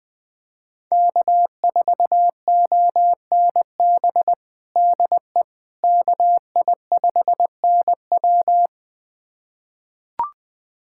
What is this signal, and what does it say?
Morse code, heard as a keyed tone, about 21 wpm, 700 Hz. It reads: K4ONB DE KI5NW
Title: USA Callsign DE Callsign 20wpm